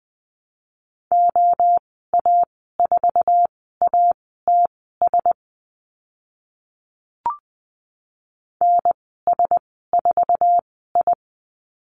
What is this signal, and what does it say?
Morse code, heard as a keyed tone, about 20 wpm, spaced as for 14 wpm, 700 Hz. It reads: OA4ATS NS4I
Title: 062 Lesson 11g - Letter h - Call Signs 20wpm